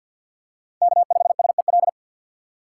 Morse code, heard as a keyed tone, about 49 wpm, 690 Hz. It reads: K5SEL
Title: USA Call Signs - 5 Characters - 6x Character Spacing - 50wpm